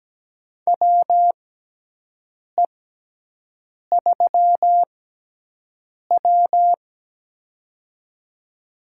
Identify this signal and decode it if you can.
Morse code, heard as a keyed tone, about 17 wpm, spaced as for 6 wpm, 700 Hz. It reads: WE3W
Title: USA Call Signs - 6x Character Spacing - 17wpm